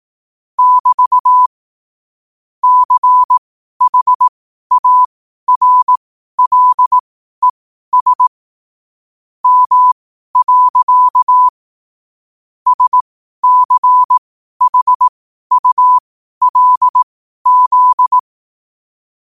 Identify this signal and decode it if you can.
Morse code, heard as a keyed tone, about 18 wpm, spaced as for 12 wpm, 1000 Hz. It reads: <BT> CHARLES M. SCHULZ
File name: qotdfiles / 12wpm